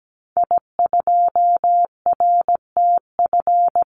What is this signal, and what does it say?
Morse code, heard as a keyed tone, about 17 wpm, 700 Hz. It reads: I2RTF
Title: Call Signs - 17wpm